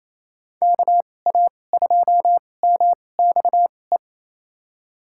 Morse code, heard as a keyed tone, about 27 wpm, spaced as for 20 wpm, 700 Hz. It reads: KA2MXE
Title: USA Call Signs - 6 Characters - 2x Character Spacing - 28wpm